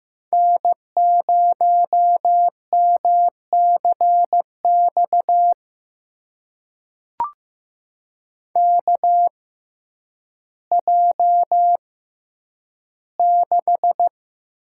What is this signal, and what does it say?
Morse code, heard as a keyed tone, about 15 wpm, spaced as for 9 wpm, 700 Hz. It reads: N0MCX KJ6
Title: USA Call Signs - 5 Characters - 6x Character Spacing - 15wpm